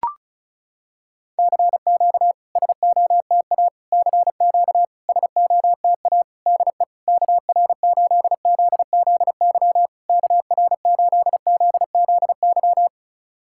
Morse code, heard as a keyed tone, about 36 wpm, 700 Hz. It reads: CQ SOTA CQ SOTA DE KR8ZZY KR8ZZY
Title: SOTA QSO Practice 35wpm